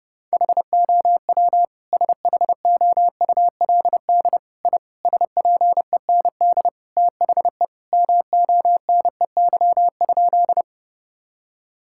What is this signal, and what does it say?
Morse code, heard as a keyed tone, about 30 wpm, 700 Hz. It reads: HOW SHOULD I SPEND THE MONEY?
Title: Sentences from Top 1000 Words - Review Part 5 - 30wpm